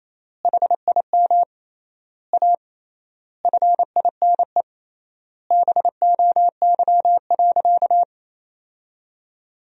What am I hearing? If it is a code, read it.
Morse code, heard as a keyed tone, about 28 wpm, 700 Hz. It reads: HIM A FINE BOY.
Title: 3x Word-Spacing - Sentences from Top 500 Words - 28wpm